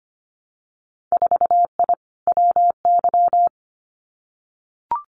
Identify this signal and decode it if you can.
Morse code, heard as a keyed tone, about 25 wpm, 700 Hz. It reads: 4I WY
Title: Winter Field Day Exchanges Encore 25wpm